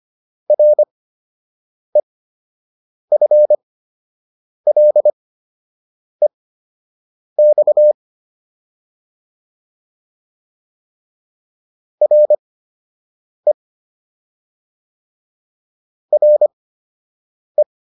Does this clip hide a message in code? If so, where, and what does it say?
Yes: Morse code from 0.5 to 17.6 s, heard as a keyed tone, about 25 wpm, spaced as for 7 wpm, 600 Hz. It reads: REFLEX RE RE